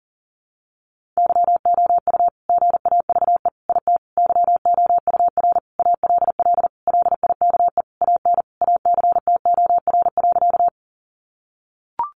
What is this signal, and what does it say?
Morse code, heard as a keyed tone, about 40 wpm, 700 Hz. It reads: YOU GAVE IT YOUR ALL LIKE AN ACTOR.
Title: Sentences from Top 1500 Words 40wpm